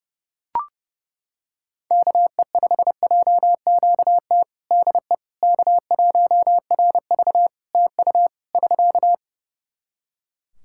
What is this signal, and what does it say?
Morse code, heard as a keyed tone, about 30 wpm, 700 Hz. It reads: KE5JQT DE K1RV TU <SK>
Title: Practice QSO - Exchange 4 - Ending - 30wpm